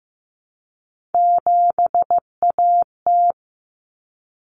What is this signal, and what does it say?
Morse code, heard as a keyed tone, about 15 wpm, 700 Hz. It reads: 7AT
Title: Common Cut Numbers - 3-Digit 15wpm